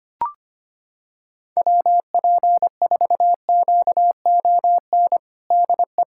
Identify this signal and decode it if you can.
Morse code, heard as a keyed tone, about 25 wpm, 700 Hz. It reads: WP4QON DE
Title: USA Callsign DE Callsign 25wpm